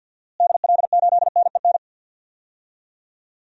Morse code, heard as a keyed tone, about 52 wpm, 700 Hz. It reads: DL8NEN
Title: DXCC-100 Call Signs - 6 Characters - Repeated 3x - 50wpm